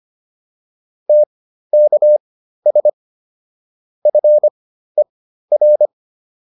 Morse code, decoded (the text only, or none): TKS FER